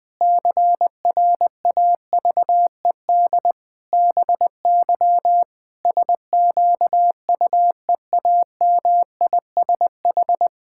Morse code, heard as a keyed tone, about 20 wpm, 700 Hz. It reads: CRAVED BY SQUEAMISH
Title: Pangram Challenge v2 - 20wpm